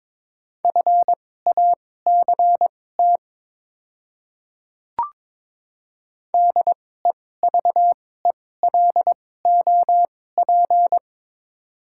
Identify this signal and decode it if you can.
Morse code, heard as a keyed tone, about 22 wpm, spaced as for 16 wpm, 690 Hz. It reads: FACT DEVELOP
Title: 2x Character Spacing - Top 500 Words - 22wpm